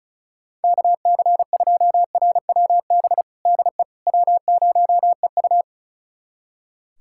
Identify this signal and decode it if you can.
Morse code, heard as a keyed tone, about 34 wpm, 700 Hz. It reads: KC2RWB DE W0EU
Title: USA Callsign DE Callsign 35wpm